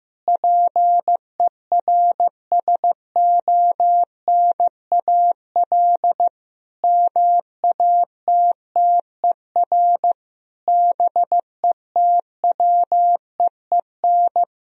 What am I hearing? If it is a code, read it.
Morse code, heard as a keyed tone, about 15 wpm, 700 Hz. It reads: PERSONAL MATTER BETWEEN
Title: Sentences from Top 1000 Words - Review Part 4 - 15wpm